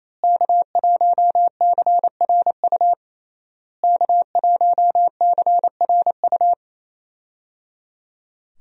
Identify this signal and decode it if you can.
Morse code, heard as a keyed tone, about 27 wpm, 700 Hz. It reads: K1CRU K1CRU
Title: USA Call Signs - 5 Characters - Repeated 3x - 28wpm